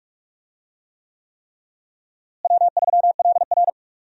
Morse code, heard as a keyed tone, about 46 wpm, 700 Hz. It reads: W3LR
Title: USA Call Signs - 4 Characters - Repeated 3x - 45wpm